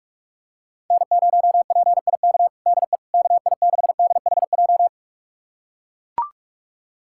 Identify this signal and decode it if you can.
Morse code, heard as a keyed tone, about 42 wpm, 700 Hz. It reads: N0PIK DE KI6DHJ
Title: USA Callsign DE Callsign 45wpm